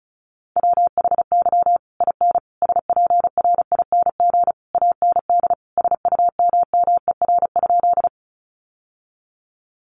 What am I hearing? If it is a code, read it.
Morse code, heard as a keyed tone, about 34 wpm, 700 Hz. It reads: WHY IN SPRING AND SUMMER?
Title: Sentences from Top 1200 Words 35wpm